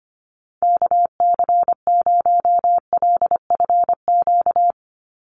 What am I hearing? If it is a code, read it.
Morse code, heard as a keyed tone, about 25 wpm, 690 Hz. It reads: KC0LFQ
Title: USA Call Signs 25wpm